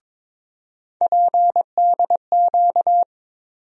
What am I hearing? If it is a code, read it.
Morse code, heard as a keyed tone, about 21 wpm, 700 Hz. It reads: PDQ
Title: USA Call Sign Postfixes - 3 Letters 22wpm